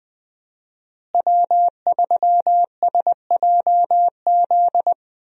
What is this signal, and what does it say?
Morse code, heard as a keyed tone, about 20 wpm, 700 Hz. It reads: W3SJZ
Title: USA Call Signs - 5 Characters - Repeated 2x - 20wpm